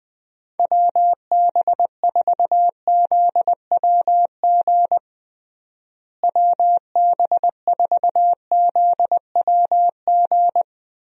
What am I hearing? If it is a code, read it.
Morse code, heard as a keyed tone, about 21 wpm, 700 Hz. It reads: WB4ZWG WB4ZWG
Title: USA Call Signs - 6 Characters - Repeated 3x - 20wpm